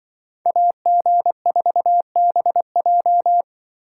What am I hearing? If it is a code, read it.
Morse code, heard as a keyed tone, about 24 wpm, 700 Hz. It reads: AG4BJ
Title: Speed Racing - USA 2x2 Call Signs 20wpm